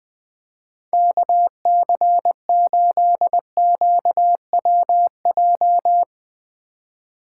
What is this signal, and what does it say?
Morse code, heard as a keyed tone, about 21 wpm, 700 Hz. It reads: KC8QWJ